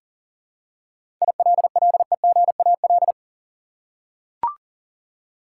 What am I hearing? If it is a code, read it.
Morse code, heard as a keyed tone, about 37 wpm, 700 Hz. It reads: ILLEGAL
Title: Seven Letter Words - 40wpm